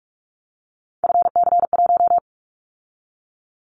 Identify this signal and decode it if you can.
Morse code, heard as a keyed tone, about 44 wpm, 700 Hz. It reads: FC1
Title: Callsign Prefixes for Top 25 DXCC - 45wpm